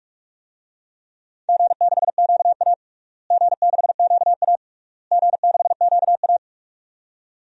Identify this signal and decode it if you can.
Morse code, heard as a keyed tone, about 42 wpm, 700 Hz. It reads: G6QA G6QA G6QA
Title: DXCC-100 Call Signs - 4 Characters - Repeated 3x - 45wpm